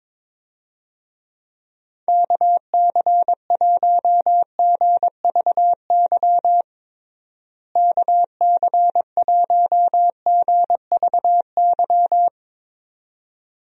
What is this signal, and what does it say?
Morse code, heard as a keyed tone, about 21 wpm, 700 Hz. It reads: KC1GVY KC1GVY
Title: USA Call Signs - 6 Characters - Repeated 2x - 22wpm